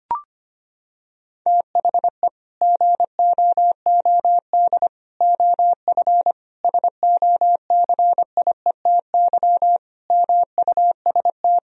Morse code, heard as a keyed tone, about 25 wpm, 700 Hz. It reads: THE GOOD OF SOCIETY MUST
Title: Sentences from Top 1000 Words - Review Part 4 - 25wpm